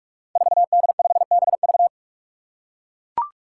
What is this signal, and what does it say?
Morse code, heard as a keyed tone, about 42 wpm, 700 Hz. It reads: 4D5BV
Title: Five Challenging Letter-Number Combinations 45wpm